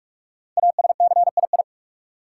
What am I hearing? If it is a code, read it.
Morse code, heard as a keyed tone, about 43 wpm, 700 Hz. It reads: ASCII
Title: Sets of 5 Words - New Word Drill - 45wpm